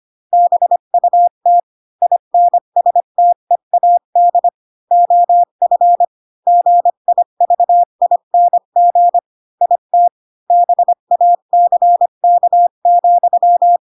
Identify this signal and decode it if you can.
Morse code, heard as a keyed tone, about 25 wpm, 700 Hz. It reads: BUT INSTEAD OF GIVING IT BACK,
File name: Pinocchio / Pinocchio ch3 20 25 ik1hge